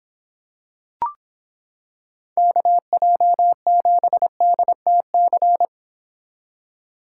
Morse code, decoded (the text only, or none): KJ7DTC